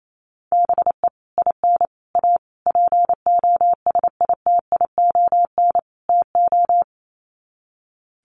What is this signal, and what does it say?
Morse code, heard as a keyed tone, about 28 wpm, 700 Hz. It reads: BE IN A POSITION TO
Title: Sets of 5 Words Encore2 28wpm